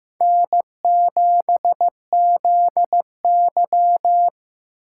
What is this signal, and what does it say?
Morse code, heard as a keyed tone, about 15 wpm, 700 Hz. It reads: N7ZY
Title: USA Call Signs - 4 Characters - 8x Character Spacing - 15wpm